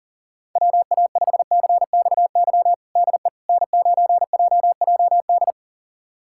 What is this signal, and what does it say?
Morse code, heard as a keyed tone, about 40 wpm, 700 Hz. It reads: WA5CXY DE N9JJD
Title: USA Callsign DE Callsign 40wpm